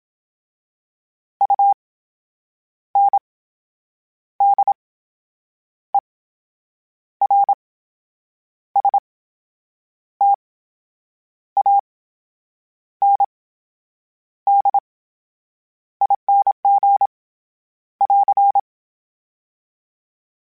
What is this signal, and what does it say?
Morse code, heard as a keyed tone, about 27 wpm, spaced as for 8 wpm, 800 Hz. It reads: UNDERSTANDING <AR>
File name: Special INGs / Special INGs-mp3s / VerySlowToFaster Ing @25